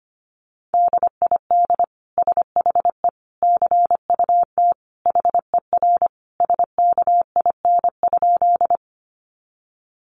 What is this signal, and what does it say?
Morse code, heard as a keyed tone, about 25 wpm, 700 Hz. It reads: DID SHE CUT HER SKIN?
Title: Sentences from Top 1300 Words 25wpm